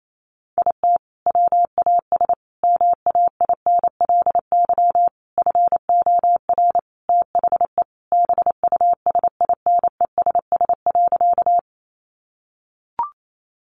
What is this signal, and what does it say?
Morse code, heard as a keyed tone, about 28 wpm, 700 Hz. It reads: IT WAS MAINLY FOR THE BUSINESS.
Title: Sentences from Top 1400 Words 28wpm